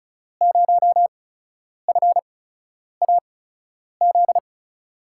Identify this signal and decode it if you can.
Morse code, heard as a keyed tone, about 36 wpm, spaced as for 10 wpm, 700 Hz. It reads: 0FAZ
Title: USA Call Signs - 8x Character Spacing - 35wpm